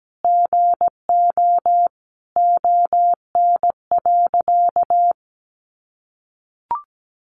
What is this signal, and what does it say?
Morse code, heard as a keyed tone, about 17 wpm, 700 Hz. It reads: GO ON.
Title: Sentences from Top 1300 Words 17wpm